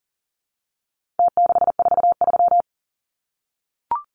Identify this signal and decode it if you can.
Morse code, heard as a keyed tone, about 40 wpm, 700 Hz. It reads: T643
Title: Numbers - Contest Style - 4-Digit 40wpm